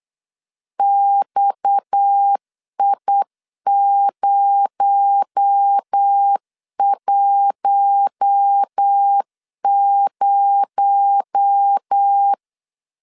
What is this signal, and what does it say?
Morse code, heard as a keyed tone, about 8 wpm, 800 Hz. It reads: XI010
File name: LEZIONI / 6 PKX0 / 6 PKX0 8wpm 3341